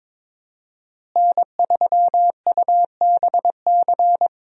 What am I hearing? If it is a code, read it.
Morse code, heard as a keyed tone, about 22 wpm, 700 Hz. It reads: N3UBC
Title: USA Call Signs - 5 Characters - Repeated 3x - 22wpm